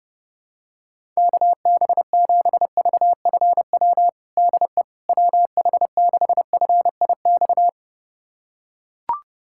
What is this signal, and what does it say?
Morse code, heard as a keyed tone, about 30 wpm, 690 Hz. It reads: KB7VFW DE WH6FIX
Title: USA Callsign DE Callsign - 6 Characters - 30wpm